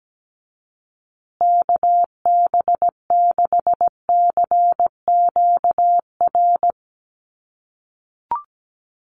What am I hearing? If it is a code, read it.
Morse code, heard as a keyed tone, about 17 wpm, 700 Hz. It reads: KB6CQR
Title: USA Call Signs Repeated 3x 17wpm